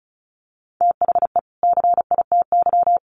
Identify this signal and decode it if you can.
Morse code, heard as a keyed tone, about 34 wpm, 700 Hz. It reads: THE CITY
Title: Sets of 2 Words 35wpm